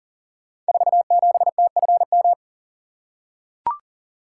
Morse code, heard as a keyed tone, about 39 wpm, 700 Hz. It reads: 47TFM
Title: Code Groups - 5 Letter:Numbers - 40wpm